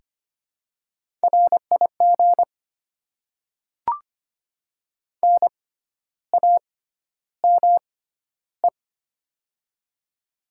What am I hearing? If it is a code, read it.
Morse code, heard as a keyed tone, about 25 wpm, spaced as for 12 wpm, 700 Hz. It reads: RIG NAME
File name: Top 20 QSO Elements - 6x Character Spacing - 25wpm